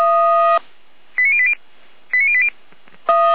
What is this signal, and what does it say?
Five-tone selective call (ZVEI), heard as quick runs of five tones, tones of about 70 ms. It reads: 89080 89080